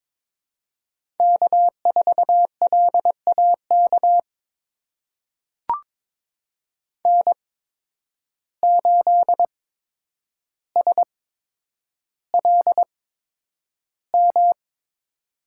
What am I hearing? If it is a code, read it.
Morse code, heard as a keyed tone, about 22 wpm, spaced as for 9 wpm, 700 Hz. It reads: K4LAK N8SLM
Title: USA Call Signs - 5 Characters - 8x Character Spacing - 22wpm